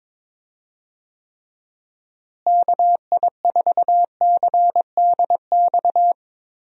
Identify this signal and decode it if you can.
Morse code, heard as a keyed tone, about 22 wpm, 700 Hz. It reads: KI4CDX
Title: USA Call Signs - 6 Characters - Repeated 2x - 22wpm